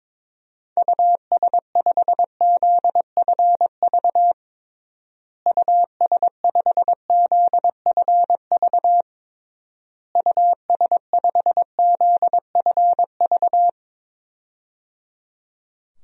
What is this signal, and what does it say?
Morse code, heard as a keyed tone, about 22 wpm, 700 Hz. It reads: US5ZFV US5ZFV US5ZFV
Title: DXCC-100 Call Signs - 6 Characters - Repeated 3x - 22wpm